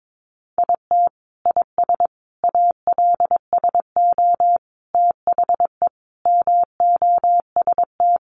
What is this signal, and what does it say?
Morse code, heard as a keyed tone, about 22 wpm, 700 Hz. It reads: IT IS ALSO THE MOST
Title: Sentences from Top 1400 Words 22wpm